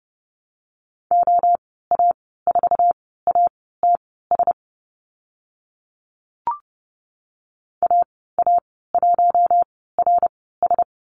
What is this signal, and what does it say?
Morse code, heard as a keyed tone, about 30 wpm, spaced as for 17 wpm, 700 Hz. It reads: OA4ATS AA1RS
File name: Lesson 14g - Number 2 - Call Signs 30wpm